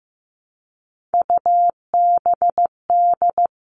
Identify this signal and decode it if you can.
Morse code, heard as a keyed tone, about 15 wpm, 700 Hz. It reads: UBD
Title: All Cut Numbers - 3-Digit 15wpm